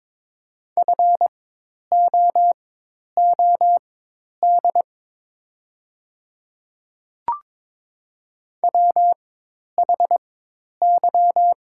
Morse code, heard as a keyed tone, about 23 wpm, spaced as for 10 wpm, 700 Hz. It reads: FOOD WHY
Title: 4x Character Spacing - Top 300 Words - 22wpm